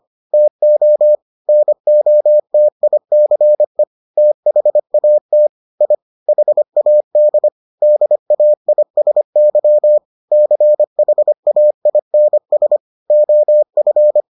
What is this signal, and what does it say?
Morse code, heard as a keyed tone, about 25 wpm, 600 Hz. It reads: TO NOTICE THAT I HAD DAISY CHAINS OF